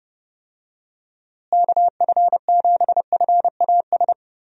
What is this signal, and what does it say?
Morse code, heard as a keyed tone, about 30 wpm, 700 Hz. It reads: KF7FAS
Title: USA Call Signs - 6 Characters - Repeated 2x - 30wpm